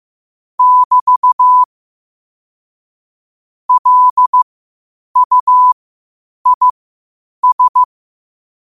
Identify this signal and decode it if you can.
Morse code, heard as a keyed tone, about 15 wpm, spaced as for 8 wpm, 1000 Hz. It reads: <BT> LUIS
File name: qotdfiles / 8wpm